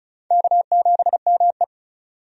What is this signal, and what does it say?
Morse code, heard as a keyed tone, about 36 wpm, 700 Hz. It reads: K7ME
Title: USA Call Signs - 4 Characters - 35wpm